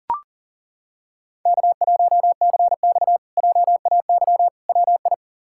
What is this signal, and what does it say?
Morse code, heard as a keyed tone, about 40 wpm, 700 Hz. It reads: K1CX JAY WI
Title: K1USN SST QSO Practice - 40wpm